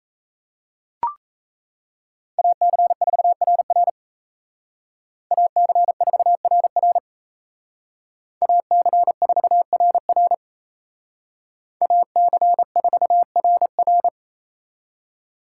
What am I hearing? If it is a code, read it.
Morse code, heard as a keyed tone, about 37 wpm, 700 Hz. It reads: AC4RR AC4RR AC4RR AC4RR